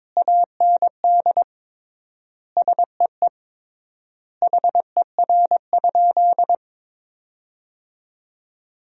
Morse code, heard as a keyed tone, about 22 wpm, 700 Hz. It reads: AND SEE HER?
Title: 3x Word-Spacing - Sentences from Top 100 Words - 22wpm